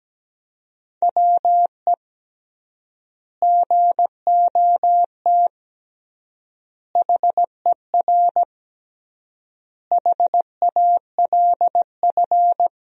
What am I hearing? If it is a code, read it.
Morse code, heard as a keyed tone, about 17 wpm, 700 Hz. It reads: WE GOT HER HALF
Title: 3x Word-Spacing - Sentences from Top 400 Words - 17wpm